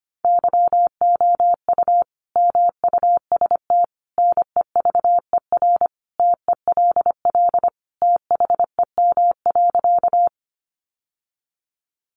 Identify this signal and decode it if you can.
Morse code, heard as a keyed tone, about 25 wpm, 700 Hz. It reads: YOU MUST NEVER TELL THEM.
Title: Sentences from Top 900 Words 25wpm